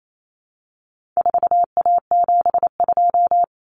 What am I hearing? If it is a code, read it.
Morse code, heard as a keyed tone, about 28 wpm, 700 Hz. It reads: 4A72
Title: Common Cut Numbers - 4-Digit 28wpm